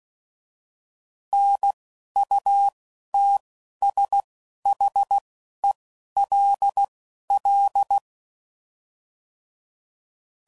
Morse code, heard as a keyed tone, about 16 wpm, spaced as for 11 wpm, 800 Hz. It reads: NUTSHELL
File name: Prefixes / By First Letter / Ns